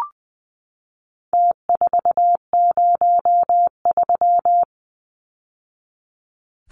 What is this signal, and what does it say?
Morse code, heard as a keyed tone, about 20 wpm, 700 Hz. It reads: T403
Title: Numbers - Contest Style - 4-Digit 20wpm